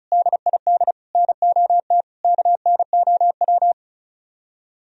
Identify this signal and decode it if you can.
Morse code, heard as a keyed tone, about 44 wpm, 700 Hz. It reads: DID NOT KNOW